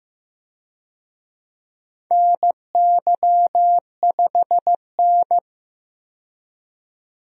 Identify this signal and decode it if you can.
Morse code, heard as a keyed tone, about 16 wpm, 700 Hz. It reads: NY5N
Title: USA Call Signs - 4 Characters - 15wpm